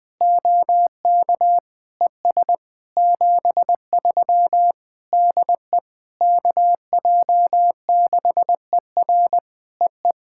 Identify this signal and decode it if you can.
Morse code, heard as a keyed tone, about 20 wpm, 700 Hz. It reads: OK ES 73 DE KJ6ER EE
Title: POTA QSO Practice - 20wpm